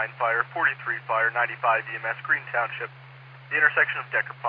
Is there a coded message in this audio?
Yes: Emergency Alert System (SAME) tones from 0.5 to 3.9 s, heard as harsh buzzing bursts, with speech.